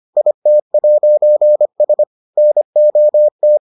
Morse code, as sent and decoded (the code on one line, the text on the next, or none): .. - .----. ... / -. --- -
IT'S NOT